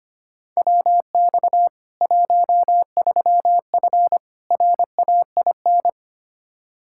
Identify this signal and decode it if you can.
Morse code, heard as a keyed tone, about 25 wpm, 700 Hz. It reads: WX 13F RAIN
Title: Practice QSO - Exchange 2 - Info - 25wpm